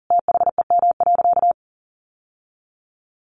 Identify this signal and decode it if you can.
Morse code, heard as a keyed tone, about 37 wpm, 700 Hz. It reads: THEM.
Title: Sentences from Top 600 Words 40wpm